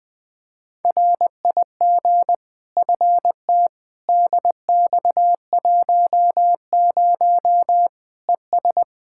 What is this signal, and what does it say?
Morse code, heard as a keyed tone, about 21 wpm, 700 Hz. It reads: RIG FT DX10 ES
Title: Practice QSO - Exchange 2 - Info - 20wpm